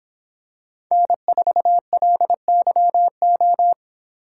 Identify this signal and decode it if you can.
Morse code, heard as a keyed tone, about 23 wpm, 700 Hz. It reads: N4LYO